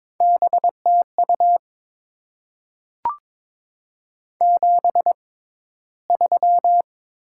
Morse code, heard as a keyed tone, about 22 wpm, spaced as for 14 wpm, 700 Hz. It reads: BTU 73
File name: Top 20 QSO Elements - 6x Character Spacing - 22wpm